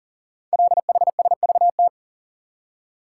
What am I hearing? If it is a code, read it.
Morse code, heard as a keyed tone, about 40 wpm, 700 Hz. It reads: LHSVT
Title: Code Groups - 5 Letters - 40wpm